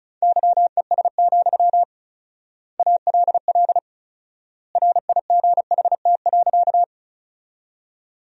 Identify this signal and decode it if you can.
Morse code, heard as a keyed tone, about 36 wpm, 700 Hz. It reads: YES, ALL RIGHT.
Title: 4x Word-Spacing - Sentences from Top 500 Words - 35wpm